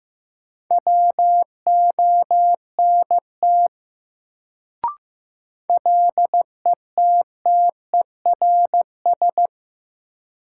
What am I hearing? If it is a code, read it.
Morse code, heard as a keyed tone, about 15 wpm, 700 Hz. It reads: WONT LETTERS
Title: Sentences from Top 1000 Words - Review Part 6 - New Word Drill - 15wpm